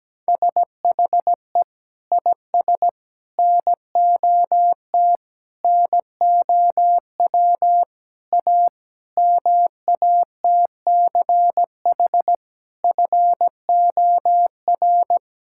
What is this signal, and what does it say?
Morse code, heard as a keyed tone, about 17 wpm, 700 Hz. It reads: SHE IS NOT NOW A MATCH FOR
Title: Sentences from Top 1600 Words 17wpm